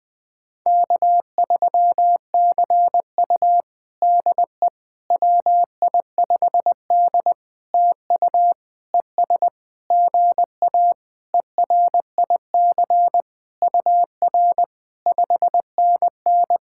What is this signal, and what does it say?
Morse code, heard as a keyed tone, about 20 wpm, 700 Hz. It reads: K3CU DE WI5D TU ES GA ERIC UR 5NN
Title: POTA QSO Practice - 20wpm